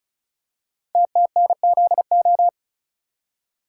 Morse code, heard as a keyed tone, about 36 wpm, 700 Hz. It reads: TTNZO
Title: Code Groups - 5 Letters - 35wpm